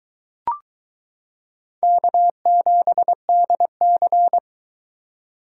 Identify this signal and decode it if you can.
Morse code, heard as a keyed tone, about 22 wpm, 700 Hz. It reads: K7DC